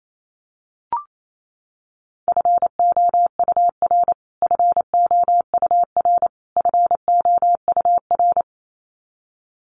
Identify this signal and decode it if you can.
Morse code, heard as a keyed tone, about 28 wpm, 700 Hz. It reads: FOUR FOUR FOUR
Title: Top 500 Words Repeated 28wpm